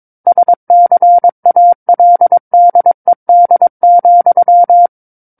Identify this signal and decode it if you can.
Morse code, heard as a keyed tone, about 22 wpm, 700 Hz. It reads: SCALDED,
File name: Book One / 13-Chapter 13 - 22 WPM